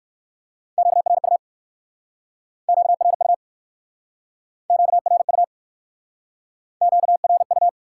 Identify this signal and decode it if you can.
Morse code, heard as a keyed tone, about 59 wpm, 700 Hz. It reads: QRU QRU QRU QRU